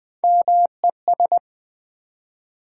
Morse code, heard as a keyed tone, about 20 wpm, 700 Hz. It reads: MES
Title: Build A Word - 1 Letter At A Time - 4-Letter Words - 20wpm